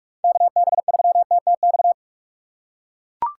Morse code, heard as a keyed tone, about 42 wpm, 700 Hz. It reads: KB3TTX
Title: USA Call Signs - 6x Character Spacing - 45wpm